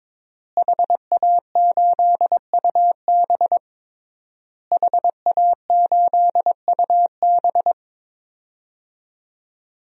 Morse code, heard as a keyed tone, about 22 wpm, 700 Hz. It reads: HA8UB HA8UB
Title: DXCC-100 Call Signs - 5 Characters - Repeated 3x - 22wpm